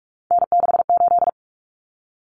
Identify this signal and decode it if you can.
Morse code, heard as a keyed tone, about 44 wpm, 700 Hz. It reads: N68
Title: Common Cut Numbers - 3-Digit 45wpm